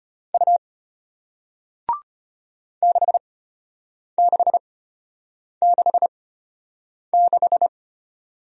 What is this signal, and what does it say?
Morse code, heard as a keyed tone, about 34 wpm, spaced as for 8 wpm, 700 Hz. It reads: U 6666